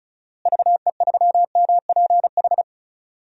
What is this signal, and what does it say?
Morse code, heard as a keyed tone, about 36 wpm, 700 Hz. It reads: VE3MPH
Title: DXCC-100 Call Signs - 6 Characters - Repeated 3x - 35wpm